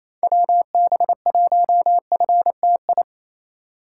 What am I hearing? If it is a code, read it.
Morse code, heard as a keyed tone, about 28 wpm, 700 Hz. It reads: WB1FTI
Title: USA Call Signs - 6 Characters - 2x Character Spacing - 28wpm